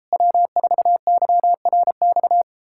Morse code, heard as a keyed tone, about 38 wpm, 700 Hz. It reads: W4YRX